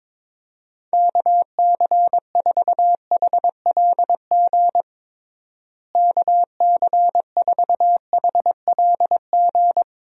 Morse code, heard as a keyed tone, about 22 wpm, 700 Hz. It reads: KC4HLG KC4HLG
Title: USA Call Signs - 6 Characters - Repeated 3x - 22wpm